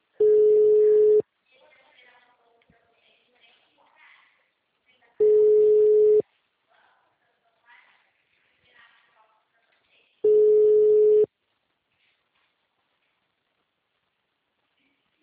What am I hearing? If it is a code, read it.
Ringback tone (Europe, 425 Hz), with speech.
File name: out- -2006- - - .17952